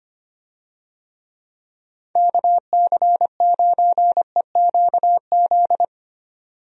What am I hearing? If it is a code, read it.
Morse code, heard as a keyed tone, about 25 wpm, 700 Hz. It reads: KC9EQZ
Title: USA Call Signs - 6 Characters - 25wpm